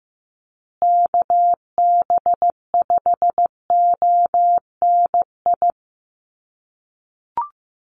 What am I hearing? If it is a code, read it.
Morse code, heard as a keyed tone, about 15 wpm, 700 Hz. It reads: KB5ONI
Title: USA Call Signs Repeated 3x 15wpm